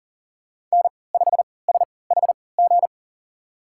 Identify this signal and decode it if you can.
Morse code, heard as a keyed tone, about 37 wpm, spaced as for 22 wpm, 700 Hz. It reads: N5SHG
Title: USA Call Signs - 5 Characters - 3x Character Spacing - 40wpm